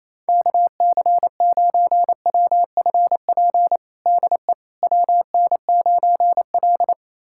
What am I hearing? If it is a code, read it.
Morse code, heard as a keyed tone, about 28 wpm, 700 Hz. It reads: KC9WFP DE WN9L
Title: USA Callsign DE Callsign 28wpm